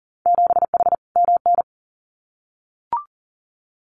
Morse code, heard as a keyed tone, about 40 wpm, 700 Hz. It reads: 7H MN
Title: Winter Field Day Exchanges Encore 40wpm